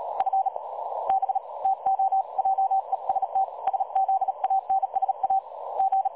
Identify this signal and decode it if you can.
Morse code, heard as a keyed tone, about 37 wpm, spaced as for 29 wpm, 770 Hz. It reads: FB TO JEST SZANSA G
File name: dlazaawansowanych / nagraniazpasma / websdr recording 3565.0kHz